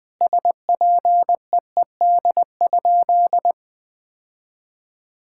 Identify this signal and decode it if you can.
Morse code, heard as a keyed tone, about 21 wpm, 700 Hz. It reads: SPEED?
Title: Sentences from Top 1000 Words - Review Part 5 - 20wpm